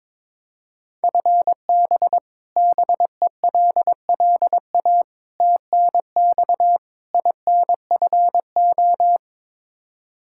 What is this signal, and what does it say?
Morse code, heard as a keyed tone, about 22 wpm, 700 Hz. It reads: FB BELLA TNX INFO
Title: Practice QSO - Exchange 3 - More Info - 22wpm